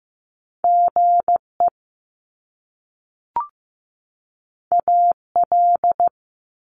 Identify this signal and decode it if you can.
Morse code, heard as a keyed tone, about 15 wpm, 700 Hz. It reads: GE AL
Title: Top Two Letter Combinations in Words 15wpm